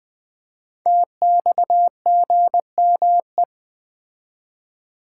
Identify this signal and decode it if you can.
Morse code, heard as a keyed tone, about 20 wpm, 700 Hz. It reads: TXGME
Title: Code Groups - 5 Letter:Numbers - 20wpm